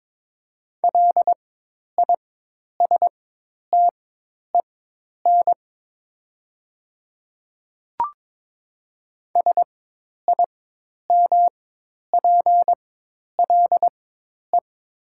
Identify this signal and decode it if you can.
Morse code, heard as a keyed tone, about 22 wpm, spaced as for 10 wpm, 700 Hz. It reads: LISTEN SIMPLE
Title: 4x Character Spacing - Top 500 Words - 22wpm